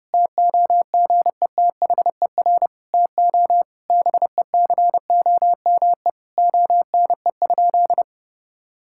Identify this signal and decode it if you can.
Morse code, heard as a keyed tone, about 32 wpm, 700 Hz. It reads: TOGETHER TO BECOME ONE?
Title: Sentences from Top 1600 Words 30wpm